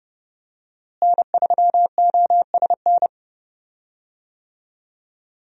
Morse code, heard as a keyed tone, about 30 wpm, 700 Hz. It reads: N3OSN
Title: USA Call Signs - 5 Characters - Repeated 2x - 30wpm